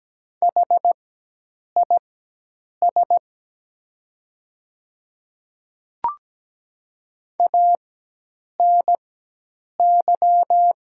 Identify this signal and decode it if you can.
Morse code, heard as a keyed tone, about 17 wpm, spaced as for 8 wpm, 700 Hz. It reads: HIS ANY
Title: 4x Character Spacing - Top 200 Words - 17wpm